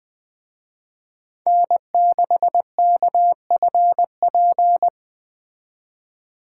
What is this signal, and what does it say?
Morse code, heard as a keyed tone, about 20 wpm, 700 Hz. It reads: N6KFP
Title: Speed Racing - USA Call Signs 17wpm